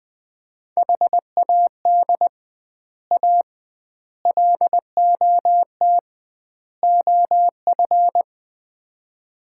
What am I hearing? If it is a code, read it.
Morse code, heard as a keyed tone, about 20 wpm, 700 Hz. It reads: HAD A LOT OF
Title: 2x Word-Spacing - Sets of 4 Words - 20wpm